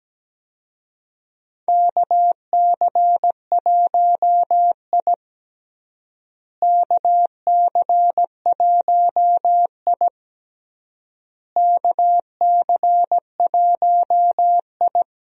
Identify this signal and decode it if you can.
Morse code, heard as a keyed tone, about 17 wpm, 700 Hz. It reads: KC1I KC1I KC1I
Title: USA Call Signs - 4 Characters - Repeated 3x - 17wpm